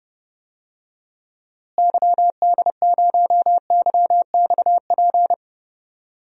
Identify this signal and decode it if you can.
Morse code, heard as a keyed tone, about 32 wpm, 700 Hz. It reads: YD0YXP
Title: DXCC-100 Call Signs - 6 Characters - Repeated 2x - 30wpm